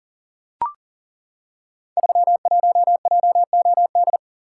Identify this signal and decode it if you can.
Morse code, heard as a keyed tone, about 39 wpm, 700 Hz. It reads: 31JOD
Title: Code Groups - 5 Letter:Numbers - 40wpm